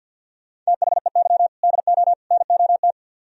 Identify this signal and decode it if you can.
Morse code, heard as a keyed tone, about 61 wpm, 700 Hz. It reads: THEY DO NOT
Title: Speed-Racing - Sets of 3 Words 50wpm